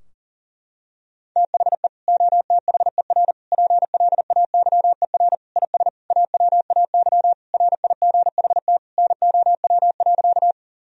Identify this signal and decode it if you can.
Morse code, heard as a keyed tone, about 40 wpm, 700 Hz. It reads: THE OTHER PLAYER IS AWAY RIGHT NOW.
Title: Sentences from Top 1000 Words - Review Part 4 - 40wpm